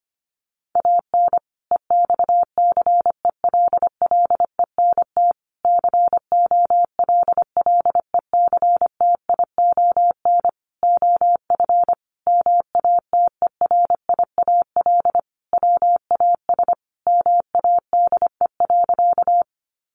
Morse code, heard as a keyed tone, about 25 wpm, 700 Hz. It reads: AN EXCELLENT COLLECTION OF MATERIAL WAS MADE.
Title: Sentences from Top 1200 Words 25wpm